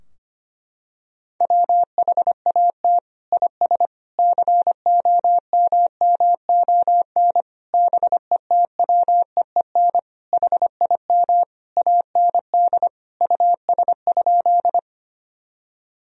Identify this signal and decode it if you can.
Morse code, heard as a keyed tone, about 25 wpm, 700 Hz. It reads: WHAT IS COMMON BETWEEN HIM AND US?
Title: Sentences from Top 1000 Words - Review Part 3 - 25wpm